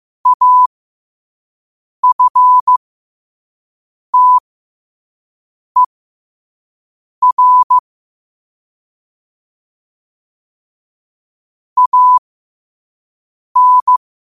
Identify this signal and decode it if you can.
Morse code, heard as a keyed tone, about 15 wpm, spaced as for 5 wpm, 1000 Hz. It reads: AFTER AN
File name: qotdfiles / 5wpm